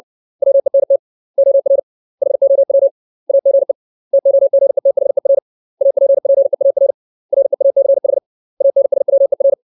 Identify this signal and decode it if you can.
Morse code, heard as a keyed tone, about 60 wpm, 550 Hz. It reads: WENT OR HOW AGE TOGETHER APPEAR REACH ANSWER
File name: 632 words / 632 Words 60WPM